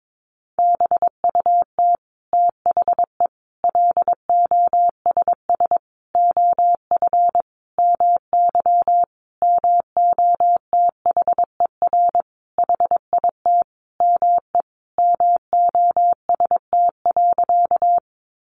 Morse code, heard as a keyed tone, about 22 wpm, 700 Hz. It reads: BUT THE LOSS OF MY MOTHER HIT ME MOST.
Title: Sentences from Top 1400 Words 22wpm